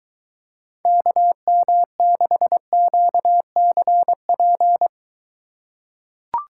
Morse code, heard as a keyed tone, about 23 wpm, 700 Hz. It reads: KM6QCP